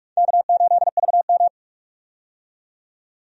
Morse code, heard as a keyed tone, about 46 wpm, 700 Hz. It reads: K8VM
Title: USA Call Signs - 4 Characters - 45wpm